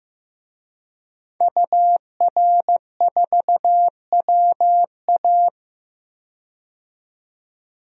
Morse code, heard as a keyed tone, about 15 wpm, 700 Hz. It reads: UR4WA
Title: DXCC-100 Call Signs - 5 Characters - 15wpm